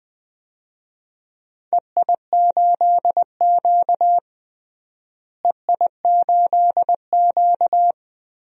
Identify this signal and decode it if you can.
Morse code, heard as a keyed tone, about 20 wpm, 700 Hz. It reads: EI8Q EI8Q
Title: DXCC-100 Call Signs - 4 Characters - Repeated 2x - 20wpm